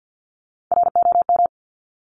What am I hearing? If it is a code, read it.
Morse code, heard as a keyed tone, about 50 wpm, 700 Hz. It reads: ROM
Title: Top Three Letter Combinations in Words 50wpm